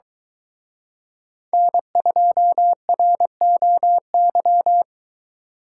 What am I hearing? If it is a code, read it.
Morse code, heard as a keyed tone, about 22 wpm, 700 Hz. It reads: N2ROY